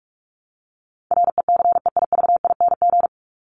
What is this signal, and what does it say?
Morse code, heard as a keyed tone, about 44 wpm, 700 Hz. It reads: RECEIVING
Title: ING Suffix - Top 100 Words with Base Words 45wpm